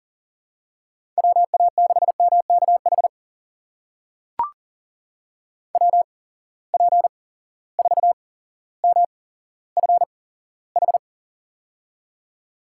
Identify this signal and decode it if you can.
Morse code, heard as a keyed tone, about 40 wpm, spaced as for 17 wpm, 700 Hz. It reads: WA6MKH WP4MFH
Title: USA Call Signs - 8x Character Spacing - 40wpm